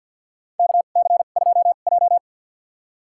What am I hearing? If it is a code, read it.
Morse code, heard as a keyed tone, about 50 wpm, spaced as for 36 wpm, 700 Hz. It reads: KC2J
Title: USA Call Signs - 2x Character Spacing - 50wpm